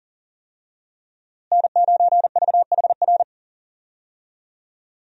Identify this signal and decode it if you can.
Morse code, heard as a keyed tone, about 40 wpm, 700 Hz. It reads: N9VHR
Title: USA Call Signs - 5 Characters - Repeated 3x - 40wpm